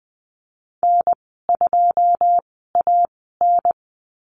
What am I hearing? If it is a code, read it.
Morse code, heard as a keyed tone, about 20 wpm, spaced as for 15 wpm, 700 Hz. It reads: N2AN
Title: 088 Lesson 15g - Number 5 - Call Signs 20wpm